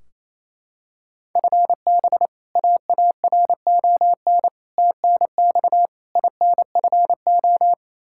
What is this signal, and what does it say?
Morse code, heard as a keyed tone, about 28 wpm, 700 Hz. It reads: FB AARON TNX INFO
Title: K1USN SST QSO Practice 28wpm